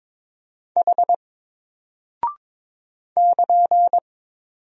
Morse code, heard as a keyed tone, about 22 wpm, 700 Hz. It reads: H <KN>
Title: Single Character (Letter:Number:Punctuation:Prosign) Alt 22wpm